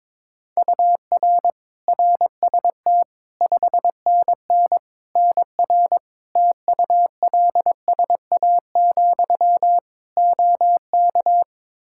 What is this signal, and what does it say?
Morse code, heard as a keyed tone, about 22 wpm, 700 Hz. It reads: UR RST 5NN NR TULSA, OK
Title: Practice QSO - Exchange 1 - Beginning - 22wpm